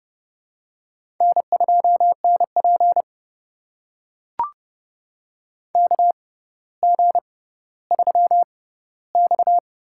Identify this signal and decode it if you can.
Morse code, heard as a keyed tone, about 30 wpm, spaced as for 15 wpm, 700 Hz. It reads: N2NP KG3X
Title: USA Call Signs - 4 Characters - 6x Character Spacing - 30wpm